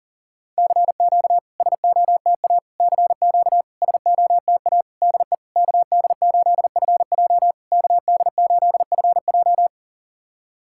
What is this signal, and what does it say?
Morse code, heard as a keyed tone, about 44 wpm, 700 Hz. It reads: CQ SOTA CQ SOTA DE KD8FJ KD8FJ
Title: SOTA QSO Practice 40wpm